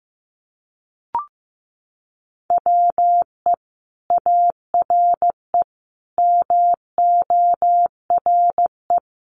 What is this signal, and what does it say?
Morse code, heard as a keyed tone, about 15 wpm, 700 Hz. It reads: WE ARE MORE
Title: Sentences from Top 500 Words 15wpm